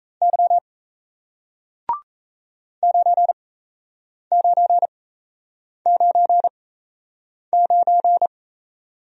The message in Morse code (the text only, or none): Y 9999